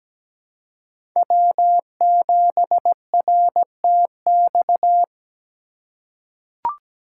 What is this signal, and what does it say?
Morse code, heard as a keyed tone, about 17 wpm, 700 Hz. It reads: W7RTX
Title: USA Call Signs - 5 Characters - 2x Character Spacing - 17wpm